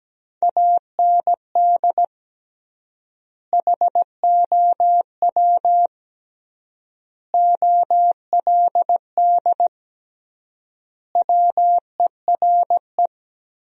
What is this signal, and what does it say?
Morse code, heard as a keyed tone, about 17 wpm, 700 Hz. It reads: AND HOW OLD WERE
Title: 3x Word-Spacing - Sentences from Top 200 Words - 17wpm